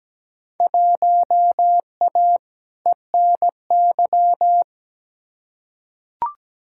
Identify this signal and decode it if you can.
Morse code, heard as a keyed tone, about 17 wpm, 700 Hz. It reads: 1A ENY
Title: Field Day Exchanges Encore - 17wpm